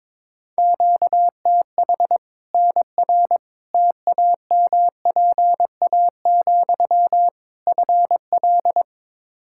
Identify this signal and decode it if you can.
Morse code, heard as a keyed tone, about 22 wpm, 700 Hz. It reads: QTH NR TAMPA, FL
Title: Practice QSO - Exchange 1 - Beginning - 22wpm